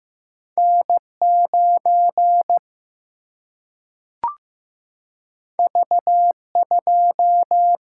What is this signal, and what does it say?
Morse code, heard as a keyed tone, about 15 wpm, 700 Hz. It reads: N 9 V 2
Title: Single-Letter Single-Number 15wpm